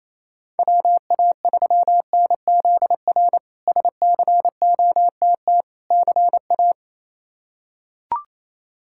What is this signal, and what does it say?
Morse code, heard as a keyed tone, about 28 wpm, 700 Hz. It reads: WA3NZR SCOTT CA
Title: K1USN SST QSO Practice - 28wpm